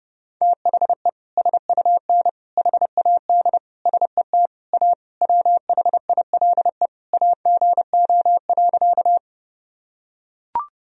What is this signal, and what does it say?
Morse code, heard as a keyed tone, about 30 wpm, 700 Hz. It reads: THE SUN HAD SET A WHILE AGO.
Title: Sentences from Top 1000 Words - Review Part 1 - 30wpm